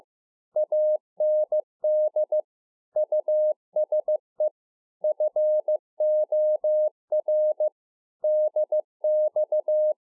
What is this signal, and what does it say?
Morse code, heard as a keyed tone, about 15 wpm, 610 Hz. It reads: AND USE FOR DX